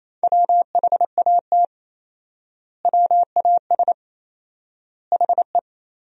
Morse code, heard as a keyed tone, about 28 wpm, 700 Hz. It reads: WHAT WAS HE
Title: 4x Word-Spacing - Sentences from Top 100 Words - 28wpm